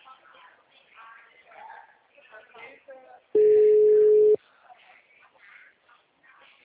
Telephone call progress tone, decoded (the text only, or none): ringback tone